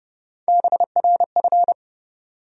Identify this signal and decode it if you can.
Morse code, heard as a keyed tone, about 32 wpm, 700 Hz. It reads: BRF
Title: USA Call Sign Postfixes - 3 Letters 30wpm